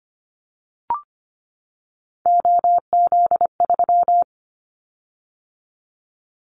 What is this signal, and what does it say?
Morse code, heard as a keyed tone, about 25 wpm, 700 Hz. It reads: OZ3
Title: Callsign Prefixes for Top 100 DXCC - 25wpm